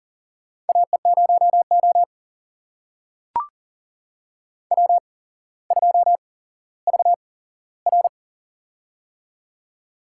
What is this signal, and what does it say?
Morse code, heard as a keyed tone, about 38 wpm, spaced as for 17 wpm, 700 Hz. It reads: AE0O W2VR
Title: USA Call Signs - 4 Characters - 8x Character Spacing - 40wpm